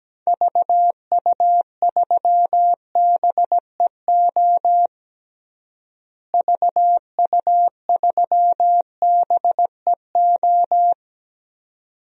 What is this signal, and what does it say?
Morse code, heard as a keyed tone, about 17 wpm, 700 Hz. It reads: VU3BEO VU3BEO
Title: DXCC-100 Call Signs - 6 Characters - Repeated 2x - 17wpm